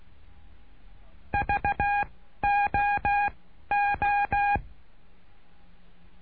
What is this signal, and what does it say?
Morse code, heard as a keyed tone, about 16 wpm, 810 Hz. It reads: V O O
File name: cursos / CURSO DE TELEGRAFIA / Telegrafia - Trilha 09